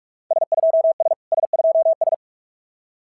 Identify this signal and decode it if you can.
Morse code, heard as a keyed tone, about 44 wpm, 650 Hz. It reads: S2S S2S
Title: SOTA QSO Practice 45wpm